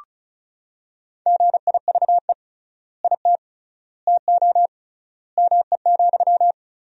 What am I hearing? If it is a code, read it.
Morse code, heard as a keyed tone, about 36 wpm, 700 Hz. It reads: GIVE IT TO ME,
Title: 3x Word-Spacing - Sentences from Top 200 Words - 35wpm